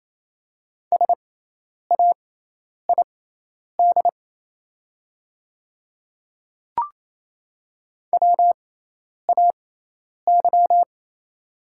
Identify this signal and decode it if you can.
Morse code, heard as a keyed tone, about 28 wpm, spaced as for 10 wpm, 700 Hz. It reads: SAID WAY
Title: 6x Character Spacing - Top 100 Words - 28wpm